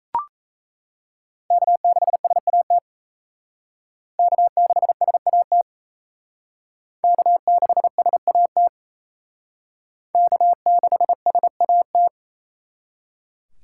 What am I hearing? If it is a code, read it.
Morse code, heard as a keyed tone, about 35 wpm, 700 Hz. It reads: K6SAT K6SAT K6SAT K6SAT
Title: Speed Racing - USA 1x3 Call Signs 28wpm